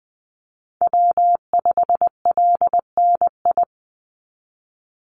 Morse code, heard as a keyed tone, about 20 wpm, 700 Hz. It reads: W5LNI
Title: Call Signs - 20wpm